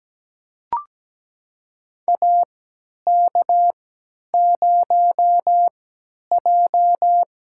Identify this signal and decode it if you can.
Morse code, heard as a keyed tone, about 17 wpm, spaced as for 10 wpm, 700 Hz. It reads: AK0J
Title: USA Call Signs - 4 Characters - 3x Character Spacing - 17wpm